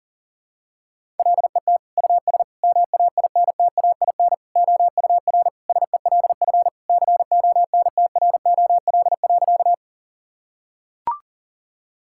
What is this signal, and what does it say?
Morse code, heard as a keyed tone, about 40 wpm, 700 Hz. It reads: LET US MAINTAIN OUR SELF CONTROL.
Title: Sentences from Top 1000 Words - Review Part 5 - 40wpm